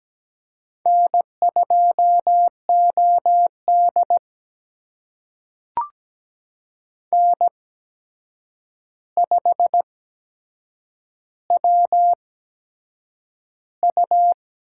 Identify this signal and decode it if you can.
Morse code, heard as a keyed tone, about 17 wpm, spaced as for 7 wpm, 700 Hz. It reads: N2OD N5WU
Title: USA Call Signs - 4 Characters - 8x Character Spacing - 17wpm